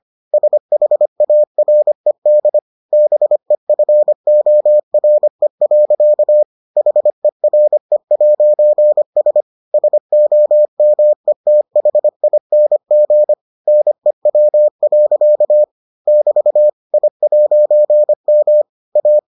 Morse code, heard as a keyed tone, about 25 wpm, 600 Hz. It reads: SHARED BEFORE. HERE'S SOMETHING NEW. <BT> I'M A